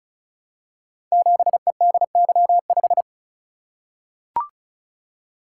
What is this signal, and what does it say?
Morse code, heard as a keyed tone, about 36 wpm, 700 Hz. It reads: 7EDY5
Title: Code Groups - 5 Letter:Numbers - 35wpm